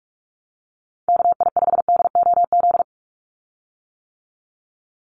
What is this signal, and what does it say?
Morse code, heard as a keyed tone, about 44 wpm, 700 Hz. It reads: KI5DOZ
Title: USA Call Signs 45wpm